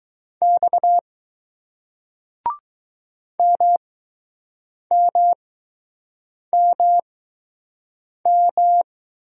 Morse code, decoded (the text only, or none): X MMMM